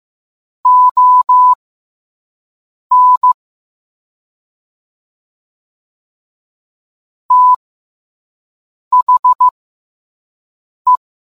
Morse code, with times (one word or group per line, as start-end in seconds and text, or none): 0.7-3.3 ON
7.3-11.0 THE